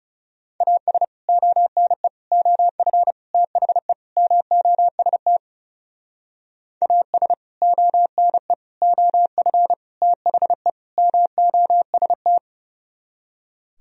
Morse code, heard as a keyed tone, about 37 wpm, 700 Hz. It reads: AS ONE OF THE MOST AS ONE OF THE MOST
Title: Speed-Racing - Sets of 5 Words 30wpm